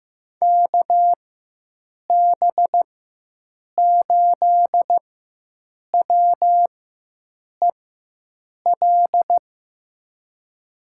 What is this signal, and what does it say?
Morse code, heard as a keyed tone, about 15 wpm, spaced as for 7 wpm, 700 Hz. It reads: KB8WEL
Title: USA Call Signs - 6 Characters - 4x Character Spacing - 15wpm